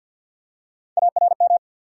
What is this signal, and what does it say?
Morse code, heard as a keyed tone, about 48 wpm, 700 Hz. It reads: ARM
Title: Three Letter Words - 50wpm